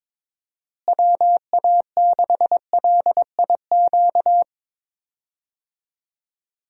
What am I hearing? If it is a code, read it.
Morse code, heard as a keyed tone, about 22 wpm, 700 Hz. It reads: WA6LIQ
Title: USA Call Signs - 6 Characters - Repeated 2x - 22wpm